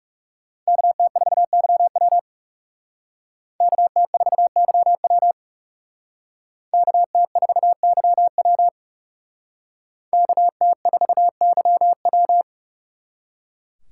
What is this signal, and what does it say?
Morse code, heard as a keyed tone, about 36 wpm, 700 Hz. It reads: KT4YW KT4YW KT4YW KT4YW
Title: Speed Racing - USA 2x2 Call Signs 30wpm